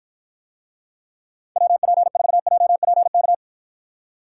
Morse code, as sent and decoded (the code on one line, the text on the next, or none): .-- .--. ....- .--- .--. -.-
WP4JPK